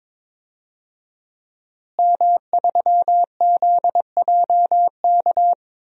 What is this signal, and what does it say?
Morse code, heard as a keyed tone, about 21 wpm, 700 Hz. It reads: M3ZJK
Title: DXCC-100 Call Signs - 5 Characters - Repeated 2x - 22wpm